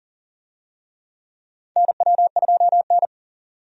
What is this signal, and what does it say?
Morse code, heard as a keyed tone, about 38 wpm, 700 Hz. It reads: NW2N
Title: USA Call Signs - 4 Characters - 40wpm